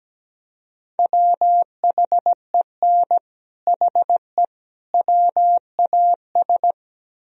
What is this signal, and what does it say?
Morse code, heard as a keyed tone, about 18 wpm, 700 Hz. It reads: WHEN HE WAS
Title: Speed-Racing - Sets of 3 Words 17wpm